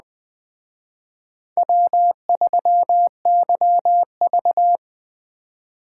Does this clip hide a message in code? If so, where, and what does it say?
Yes: Morse code from 1.6 to 4.8 s, heard as a keyed tone, about 21 wpm, 700 Hz. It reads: W3YV